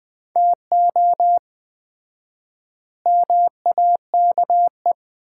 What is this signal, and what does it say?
Morse code, heard as a keyed tone, about 20 wpm, 700 Hz. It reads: TO MAKE
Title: 4x Word-Spacing - Sets of 4 Words - 20wpm